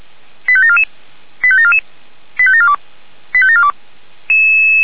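Five-tone selective call (ZVEI), heard as quick runs of five tones, tones of about 70 ms. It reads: 85744 85744 85742 85742
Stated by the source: Received on SDR, ZVEI five-tone code 85753.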